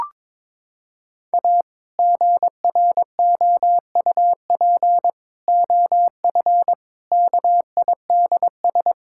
Morse code, heard as a keyed tone, about 22 wpm, 700 Hz. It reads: A GROUP OF KIDS
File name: practice / 22wpm / Sentences with 3-Word Phrases - 22wpm